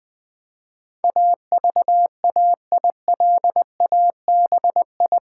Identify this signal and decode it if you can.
Morse code, heard as a keyed tone, about 20 wpm, 700 Hz. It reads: AVAILABI
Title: Build A Word - 4 Letters At A Time - 12-Letter Words - 20wpm